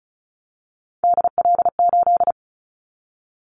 Morse code, heard as a keyed tone, about 34 wpm, 700 Hz. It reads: DL8
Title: Callsign Prefixes for Top 25 DXCC - Repeated 3x - 35wpm